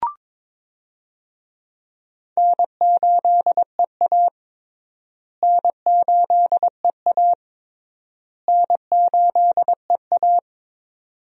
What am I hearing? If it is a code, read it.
Morse code, heard as a keyed tone, about 22 wpm, 700 Hz. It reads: N8EA N8EA N8EA
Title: USA Call Signs - 4 Characters - Repeated 3x - 22wpm